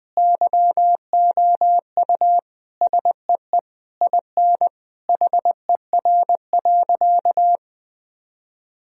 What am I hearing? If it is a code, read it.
Morse code, heard as a keyed tone, about 20 wpm, 700 Hz. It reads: YOU SEE IN HER.
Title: Sentences from Top 1000 Words - Review Part 4 - 20wpm